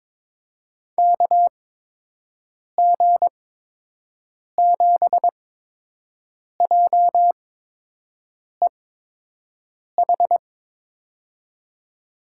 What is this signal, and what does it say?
Morse code, heard as a keyed tone, about 22 wpm, spaced as for 6 wpm, 700 Hz. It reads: KG7JEH
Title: USA Call Signs - 8x Character Spacing - 22wpm